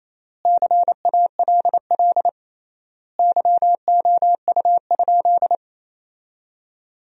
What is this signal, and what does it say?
Morse code, heard as a keyed tone, about 28 wpm, 700 Hz. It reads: CALL YOU?
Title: 3x Word-Spacing - Sentences from Top 100 Words - 28wpm